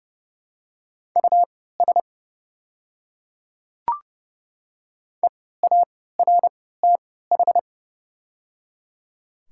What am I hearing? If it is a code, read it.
Morse code, heard as a keyed tone, about 31 wpm, spaced as for 17 wpm, 700 Hz. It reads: US EARTH
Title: 3x Character Spacing - Top 300 Words - 30wpm